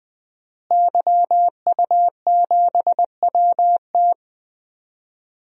Morse code, heard as a keyed tone, about 20 wpm, 700 Hz. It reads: YU7WT
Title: DXCC-100 Call Signs - 5 Characters - Repeated 2x - 20wpm